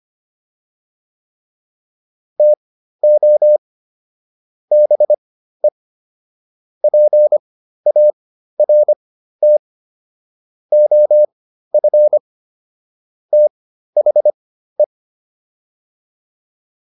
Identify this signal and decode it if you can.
Morse code, heard as a keyed tone, about 25 wpm, spaced as for 13 wpm, 600 Hz. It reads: TO BE PART OF THE